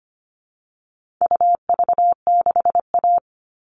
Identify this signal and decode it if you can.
Morse code, heard as a keyed tone, about 25 wpm, 700 Hz. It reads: UV6A
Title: All Cut Numbers - 4-Digit 25wpm